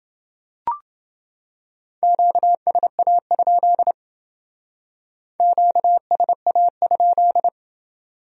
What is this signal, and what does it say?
Morse code, heard as a keyed tone, about 29 wpm, 700 Hz. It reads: QSA? QSA?